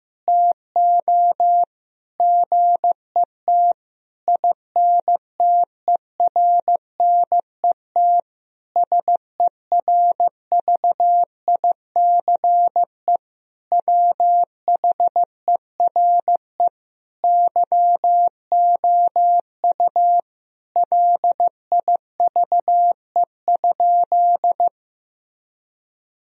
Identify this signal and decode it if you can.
Morse code, heard as a keyed tone, about 15 wpm, 700 Hz. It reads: TO GET INTERNET SERVICE WHERE YOU LIVE?
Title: Sentences from Top 1000 Words - Review Part 4 - 15wpm